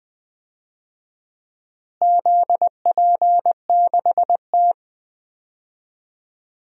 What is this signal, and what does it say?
Morse code, heard as a keyed tone, about 20 wpm, 700 Hz. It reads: ZP6T
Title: DXCC-100 Call Signs - 4 Characters - Repeated 3x - 20wpm